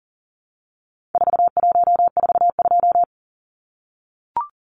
Morse code, heard as a keyed tone, about 40 wpm, 700 Hz. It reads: 4142
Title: Numbers - 4-Digit 40wpm